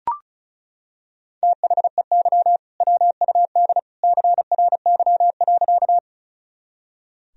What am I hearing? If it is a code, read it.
Morse code, heard as a keyed tone, about 34 wpm, 700 Hz. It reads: THEY WUD CRY.
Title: Sentences from Top 500 Words and Abbreviations 35wpm